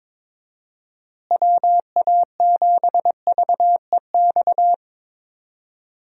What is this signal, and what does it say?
Morse code, heard as a keyed tone, about 21 wpm, 700 Hz. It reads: WA7VEX
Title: USA Call Signs - 6 Characters - Repeated 2x - 22wpm